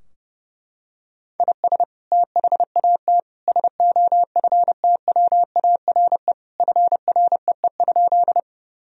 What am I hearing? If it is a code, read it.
Morse code, heard as a keyed tone, about 32 wpm, 700 Hz. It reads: IS THAT SOFTWARE FREE?
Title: Sentences from Top 1000 Words - Review Part 3 - 30wpm